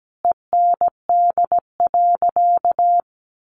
Morse code, heard as a keyed tone, about 17 wpm, 700 Hz. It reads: END.
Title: Easy Phrases 17wpm